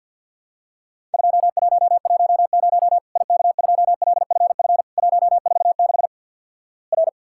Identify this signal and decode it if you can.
Morse code, heard as a keyed tone, about 50 wpm, 700 Hz. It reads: 2110 IK2LFF 146 R
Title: Worked All Europe DX Contest - QTC Exchange - 50wpm